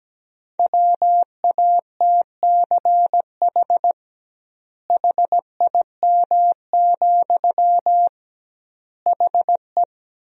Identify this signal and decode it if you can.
Morse code, heard as a keyed tone, about 17 wpm, 700 Hz. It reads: WATCH HIM, HE
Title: 2x Word-Spacing - Sentences from Top 300 Words - 17wpm